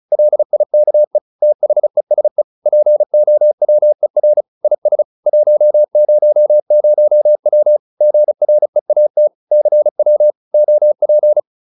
Morse code, heard as a keyed tone, about 34 wpm, 600 Hz. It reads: LIKE THESE POWER IS 100W GREAT CW OP